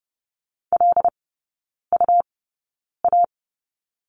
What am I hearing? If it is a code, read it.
Morse code, heard as a keyed tone, about 30 wpm, spaced as for 9 wpm, 700 Hz. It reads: LUA
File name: Lesson 17e - Letter u - Three Characters 30wpm